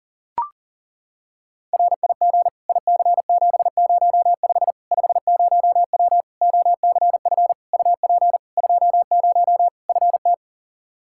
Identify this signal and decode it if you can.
Morse code, heard as a keyed tone, about 39 wpm, 700 Hz. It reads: RIG IC705 50W OCF UP 20 FT
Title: Practice QSO - Exchange 2 - Info - 40wpm